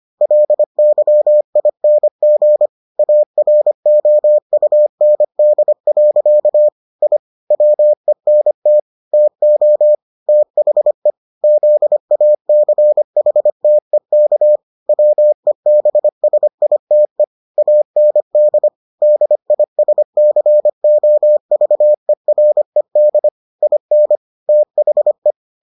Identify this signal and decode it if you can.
Morse code, heard as a keyed tone, about 25 wpm, 600 Hz. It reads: LYING AROUND. I WENT TO THE ZACHTEK WEBSITE AND DISCOVERED IN THE